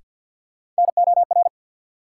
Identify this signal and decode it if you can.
Morse code, heard as a keyed tone, about 48 wpm, 700 Hz. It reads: NOR
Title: Sentences from Top 1500 Words - New Word Drill Alt - 50wpm